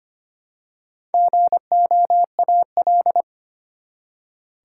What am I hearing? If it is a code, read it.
Morse code, heard as a keyed tone, about 25 wpm, 700 Hz. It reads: GOAL
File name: practice / 25wpm / Easy Words with Hints - 25wpm